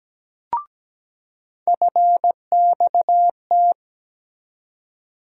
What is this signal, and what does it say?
Morse code, heard as a keyed tone, about 17 wpm, 700 Hz. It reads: FXT
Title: USA Call Sign Postfixes - 3 Letters 17wpm